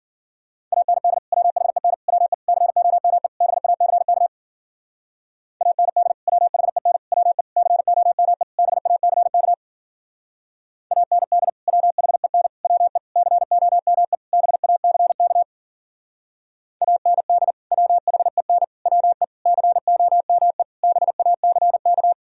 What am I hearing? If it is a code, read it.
Morse code, heard as a keyed tone, about 52 wpm, 700 Hz. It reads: AND WHEN WE COME BACK AND WHEN WE COME BACK AND WHEN WE COME BACK AND WHEN WE COME BACK